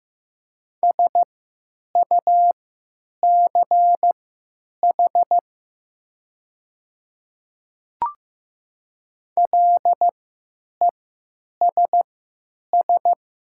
Morse code, heard as a keyed tone, about 15 wpm, spaced as for 9 wpm, 700 Hz. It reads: SUCH LESS
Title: 3x Character Spacing - Top 500 Words - 15wpm